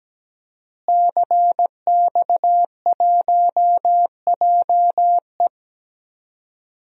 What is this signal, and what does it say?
Morse code, heard as a keyed tone, about 17 wpm, 700 Hz. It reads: CX1JE
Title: DXCC-100 Call Signs - 5 Characters - Repeated 3x - 17wpm